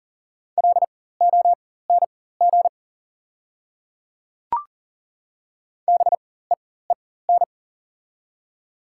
Morse code, heard as a keyed tone, about 40 wpm, spaced as for 19 wpm, 700 Hz. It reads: LONG BEEN
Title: 4x Character Spacing - Top 300 Words - 40wpm